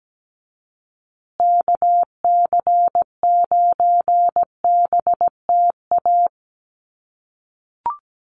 Morse code, heard as a keyed tone, about 17 wpm, 700 Hz. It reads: KC9BTA
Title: USA Call Signs Repeated 2x 17wpm